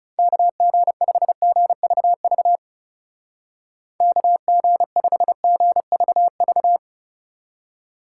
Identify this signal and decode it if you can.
Morse code, heard as a keyed tone, about 37 wpm, 700 Hz. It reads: KG5GVV KG5GVV
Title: Speed Racing - USA 2x3 Call Signs 30wpm